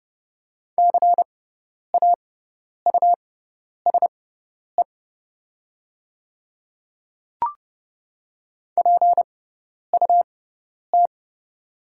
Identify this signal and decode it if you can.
Morse code, heard as a keyed tone, about 30 wpm, spaced as for 10 wpm, 690 Hz. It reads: CAUSE PUT
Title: 6x Character Spacing - Top 300 Words - 30wpm